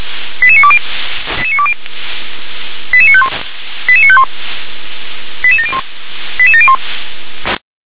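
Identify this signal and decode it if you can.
Five-tone selective call (ZVEI), heard as quick runs of five tones, tones of about 70 ms. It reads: 88920 88951 88951 88701 88701
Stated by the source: ZVEI five-tone selective call 88953, received on SDR.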